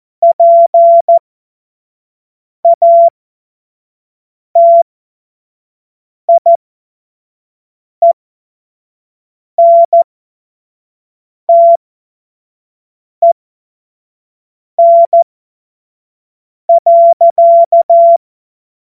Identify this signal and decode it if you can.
Morse code, heard as a keyed tone, about 14 wpm, spaced as for 5 wpm, 680 Hz. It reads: PATIENTEN.
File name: Morsekurs / Witze / Witz 02 April 2024 25 F70